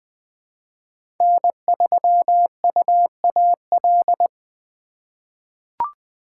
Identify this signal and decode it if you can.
Morse code, heard as a keyed tone, about 20 wpm, 700 Hz. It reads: N3UAL
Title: USA Call Signs - 5 Characters - 3x Character Spacing - 20wpm